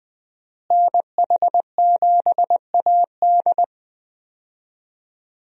Morse code, heard as a keyed tone, about 20 wpm, 700 Hz. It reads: NH7AD
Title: USA Call Signs - 5 Characters - Repeated 3x - 20wpm